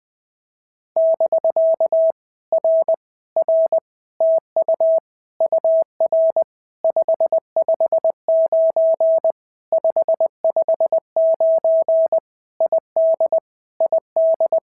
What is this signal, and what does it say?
Morse code, heard as a keyed tone, about 20 wpm, 650 Hz. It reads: <BK> R R TU UR 559 559 ID ID
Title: POTA QSO Practice - 20wpm